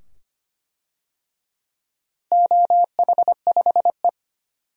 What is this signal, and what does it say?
Morse code, heard as a keyed tone, about 25 wpm, 700 Hz. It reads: OH5E
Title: DXCC-100 Call Signs - 4 Characters - Repeated 2x - 25wpm